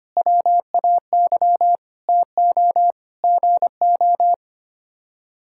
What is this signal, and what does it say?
Morse code, heard as a keyed tone, about 25 wpm, 700 Hz. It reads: WAY TO GO
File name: Sets of 9 Words - 25wpm